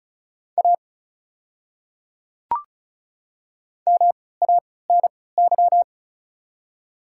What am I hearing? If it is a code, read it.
Morse code, heard as a keyed tone, about 34 wpm, spaced as for 20 wpm, 700 Hz. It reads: A MANY
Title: 3x Character Spacing - Top 100 Words - 35wpm